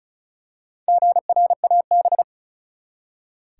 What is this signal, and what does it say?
Morse code, heard as a keyed tone, about 36 wpm, 700 Hz. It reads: GRAB
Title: Four Letter Words - 35wpm